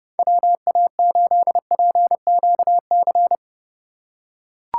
Morse code, heard as a keyed tone, about 28 wpm, 700 Hz. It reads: WA8PQC